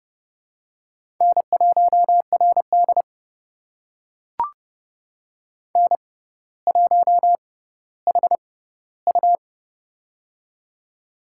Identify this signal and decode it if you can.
Morse code, heard as a keyed tone, about 32 wpm, spaced as for 16 wpm, 700 Hz. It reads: N1RD N1HU
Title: USA Call Signs - 6x Character Spacing - 30wpm